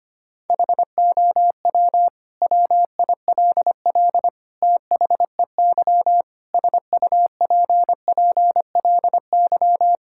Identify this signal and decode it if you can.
Morse code, heard as a keyed tone, about 25 wpm, 700 Hz. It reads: HOW WILL THEY SUPPLY
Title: Sentences from Top 1000 Words - Review Part 5 - 25wpm